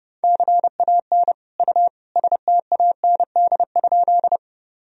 Morse code, heard as a keyed tone, about 32 wpm, 700 Hz. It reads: CAN U STAND?
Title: Sentences from Top 300 Words and Abbreviations 30wpm